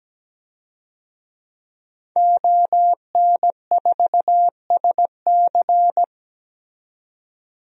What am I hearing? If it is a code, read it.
Morse code, heard as a keyed tone, about 17 wpm, 700 Hz. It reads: ON4SC
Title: DXCC-100 Call Signs - 5 Characters - 17wpm